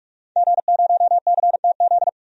Morse code, heard as a keyed tone, about 46 wpm, 700 Hz. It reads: G0CTZ
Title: DXCC-100 Call Signs - 5 Characters - 45wpm